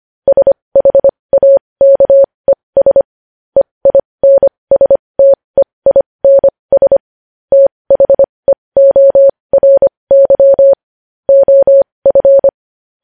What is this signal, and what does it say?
Morse code, heard as a keyed tone, about 25 wpm, spaced as for 20 wpm, 550 Hz. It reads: SHAKES EINSTEINS THEORY OF
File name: podcast / News headlines-1x-25x20